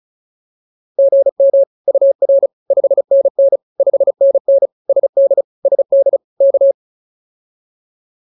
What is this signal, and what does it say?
Morse code, heard as a keyed tone, about 34 wpm, 560 Hz. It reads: GM UR 5NN 5NN SD SD K